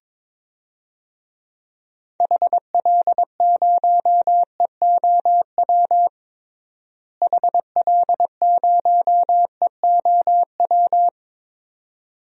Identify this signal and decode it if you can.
Morse code, heard as a keyed tone, about 21 wpm, 700 Hz. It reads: HL0EOW HL0EOW
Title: DXCC-100 Call Signs - 6 Characters - Repeated 2x - 22wpm